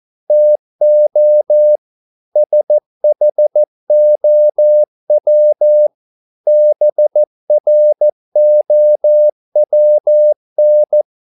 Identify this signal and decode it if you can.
Morse code, heard as a keyed tone, about 14 wpm, 600 Hz. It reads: TO SHOW BROWN